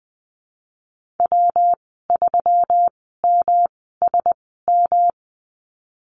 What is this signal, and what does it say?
Morse code, heard as a keyed tone, about 20 wpm, spaced as for 15 wpm, 700 Hz. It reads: W3MSM
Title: 122 Lesson 20g - Number 3 - Call Signs 20wpm